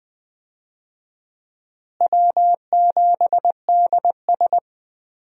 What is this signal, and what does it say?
Morse code, heard as a keyed tone, about 20 wpm, 690 Hz. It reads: W7DS
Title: USA Call Signs - 4 Characters - Repeated 3x - 20wpm